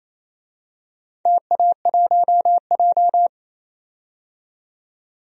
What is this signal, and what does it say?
Morse code, heard as a keyed tone, about 27 wpm, 700 Hz. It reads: TA1J
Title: DXCC-100 Call Signs - 4 Characters - 28wpm